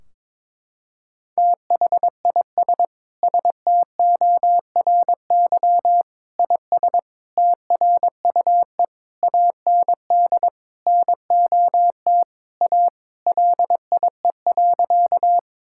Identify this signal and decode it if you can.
Morse code, heard as a keyed tone, about 22 wpm, 700 Hz. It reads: THIS STORY IS TRUE AND NOT A LIE.
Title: Sentences from Top 1600 Words 22wpm